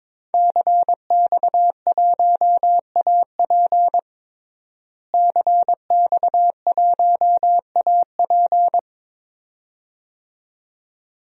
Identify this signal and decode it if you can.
Morse code, heard as a keyed tone, about 22 wpm, 700 Hz. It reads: CX1AP CX1AP
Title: DXCC-100 Call Signs - 5 Characters - Repeated 2x - 22wpm